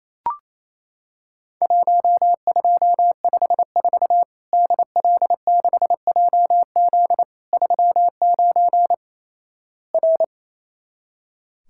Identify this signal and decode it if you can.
Morse code, heard as a keyed tone, about 28 wpm, 700 Hz. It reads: 1254 DL6JZ 39 R
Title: Worked All Europe DX Contest - QTC Exchange - 28wpm